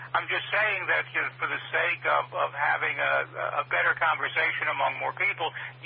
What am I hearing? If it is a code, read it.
Emergency Alert System (SAME) tones, heard as harsh buzzing bursts, with speech. It is heard from 0.8 to 4.7 s.